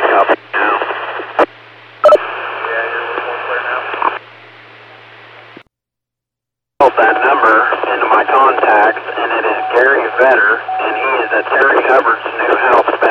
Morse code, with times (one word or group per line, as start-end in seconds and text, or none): none